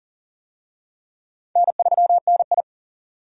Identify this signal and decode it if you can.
Morse code, heard as a keyed tone, about 40 wpm, 700 Hz. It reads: N3NI
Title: USA Call Signs - 4 Characters - 40wpm